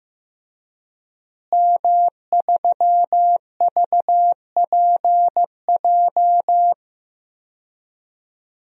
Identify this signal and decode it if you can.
Morse code, heard as a keyed tone, about 15 wpm, 700 Hz. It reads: M3VPJ
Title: DXCC-100 Call Signs - 5 Characters - Repeated 2x - 15wpm